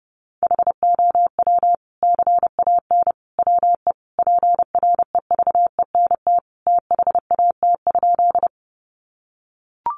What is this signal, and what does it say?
Morse code, heard as a keyed tone, about 30 wpm, 700 Hz. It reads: HOW CAN WE PREVENT THAT?
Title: Sentences from Top 1100 Words 30wpm